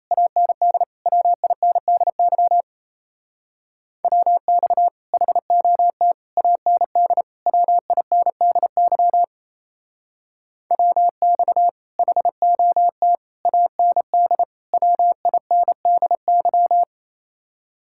Morse code, heard as a keyed tone, about 34 wpm, 700 Hz. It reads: AND WINDY WX HOT AND WINDY WX HOT AND WINDY
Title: Speed-Racing Common QSO Phrases Encore - 28wpm